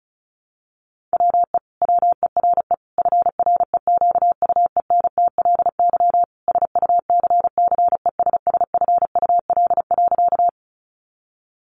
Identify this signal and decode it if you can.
Morse code, heard as a keyed tone, about 34 wpm, 700 Hz. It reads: WE WERE FREQUENTLY SUCCESSFUL.
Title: Sentences from Top 1300 Words 35wpm